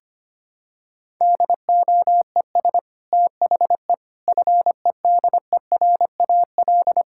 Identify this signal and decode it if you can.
Morse code, heard as a keyed tone, about 25 wpm, 700 Hz. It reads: DOES THE FEDERAL
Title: Sentences from Top 1000 Words - Review Part 5 - 25wpm